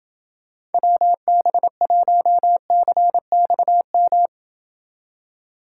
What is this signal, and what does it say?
Morse code, heard as a keyed tone, about 29 wpm, 700 Hz. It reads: WB1CXM